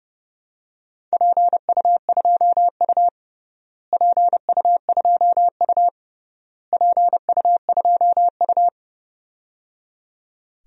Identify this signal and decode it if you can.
Morse code, heard as a keyed tone, about 30 wpm, 700 Hz. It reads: PU2U PU2U PU2U
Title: DXCC-100 Call Signs - 4 Characters - Repeated 3x - 30wpm